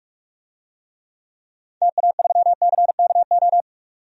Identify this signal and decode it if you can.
Morse code, heard as a keyed tone, about 46 wpm, 700 Hz. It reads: TA3CKO
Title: DXCC-100 Call Signs - 6 Characters - 45wpm